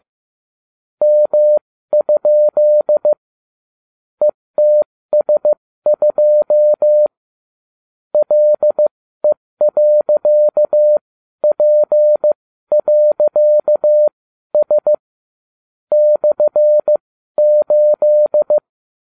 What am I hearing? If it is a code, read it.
Morse code, heard as a keyed tone, about 15 wpm, spaced as for 12 wpm, 600 Hz. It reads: M? ETS2 LE.P.S /8